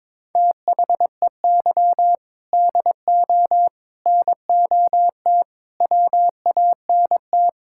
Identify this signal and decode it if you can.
Morse code, heard as a keyed tone, about 22 wpm, 700 Hz. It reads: THEY DO NOT WANT
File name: practice / 22wpm / Sentences with 3-Word Phrases - 22wpm